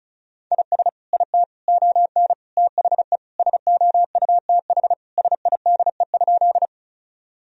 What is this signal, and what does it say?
Morse code, heard as a keyed tone, about 36 wpm, 700 Hz. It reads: IS IT ON THE SOUTH SIDE?
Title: Sentences from Top 1000 Words - Review Part 2 - 35wpm